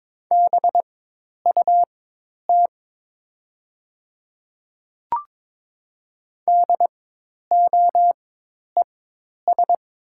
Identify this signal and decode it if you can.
Morse code, heard as a keyed tone, about 21 wpm, spaced as for 10 wpm, 700 Hz. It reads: BUT DOES
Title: 4x Character Spacing - Top 300 Words - 22wpm